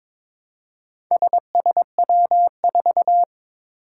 Morse code, heard as a keyed tone, about 22 wpm, 700 Hz. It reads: SSW4
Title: Four Challenging Letter-Number Combinations 22wpm